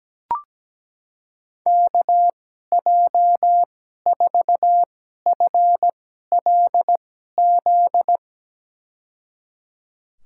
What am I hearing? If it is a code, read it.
Morse code, heard as a keyed tone, about 17 wpm, spaced as for 12 wpm, 700 Hz. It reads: KJ4FLZ
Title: USA Call Signs - 6 Characters - 2x Character Spacing - 17wpm